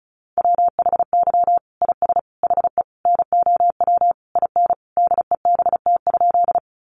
Morse code, heard as a keyed tone, about 34 wpm, 700 Hz. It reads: WHY IS HE NOW IN DEBT?
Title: Sentences from Top 1300 Words 35wpm